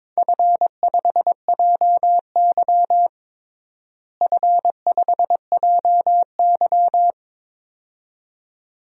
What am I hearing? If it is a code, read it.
Morse code, heard as a keyed tone, about 22 wpm, 700 Hz. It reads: F5JY F5JY
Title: DXCC-100 Call Signs - 4 Characters - Repeated 2x - 22wpm